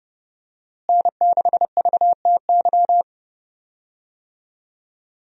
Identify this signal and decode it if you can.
Morse code, heard as a keyed tone, about 32 wpm, 700 Hz. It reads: N6VTY
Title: USA Call Signs - 5 Characters - Repeated 3x - 30wpm